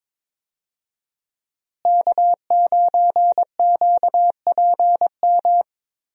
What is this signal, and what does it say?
Morse code, heard as a keyed tone, about 22 wpm, 700 Hz. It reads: K9QPM
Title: USA Call Signs - 5 Characters - Repeated 2x - 22wpm